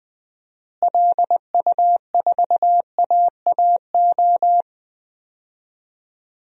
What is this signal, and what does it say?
Morse code, heard as a keyed tone, about 21 wpm, 700 Hz. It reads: LU4AAO
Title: DXCC-100 Call Signs - 6 Characters - Repeated 3x - 20wpm